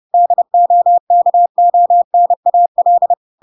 Morse code, heard as a keyed tone, about 30 wpm, 700 Hz. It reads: DOKONAL